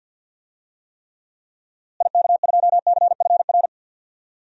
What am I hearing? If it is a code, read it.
Morse code, heard as a keyed tone, about 47 wpm, 700 Hz. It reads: IK2CFR
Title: DXCC-100 Call Signs - 6 Characters - Repeated 2x - 50wpm